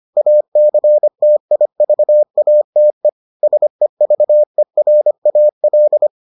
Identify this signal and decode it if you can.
Morse code, heard as a keyed tone, about 25 wpm, 600 Hz. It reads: ACTIVATE SEVERAL